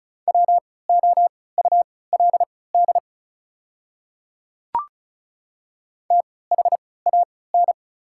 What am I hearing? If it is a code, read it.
Morse code, heard as a keyed tone, about 36 wpm, spaced as for 20 wpm, 690 Hz. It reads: WOULD THAN
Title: 3x Character Spacing - Top 100 Words - 35wpm